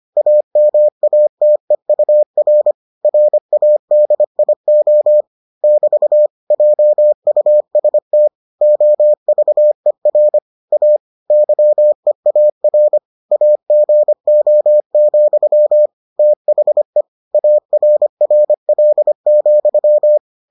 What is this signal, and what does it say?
Morse code, heard as a keyed tone, about 25 wpm, 600 Hz. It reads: AMATEUR RADIO <BT> JUST OVER A YEAR AGO, THE ARRL,